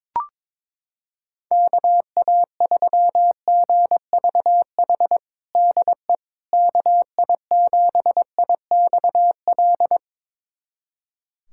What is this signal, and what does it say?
Morse code, heard as a keyed tone, about 22 wpm, 700 Hz. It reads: KA3GVH DE KI7IXL
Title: USA Callsign DE Callsign - 6 Characters - 22wpm